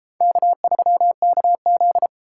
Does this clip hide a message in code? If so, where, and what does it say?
Yes: Morse code from 0.2 to 2.1 s, heard as a keyed tone, about 32 wpm, 700 Hz. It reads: K3KZ